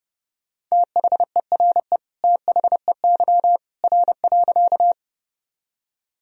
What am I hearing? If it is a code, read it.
Morse code, heard as a keyed tone, about 30 wpm, 700 Hz. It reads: THERE THEY R.
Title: Sentences from Top 100 Words and Abbreviations 30wpm